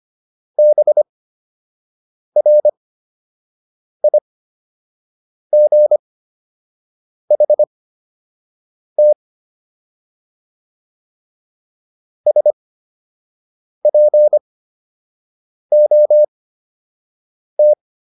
Morse code, heard as a keyed tone, about 25 wpm, spaced as for 6 wpm, 600 Hz. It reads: BRIGHT SPOT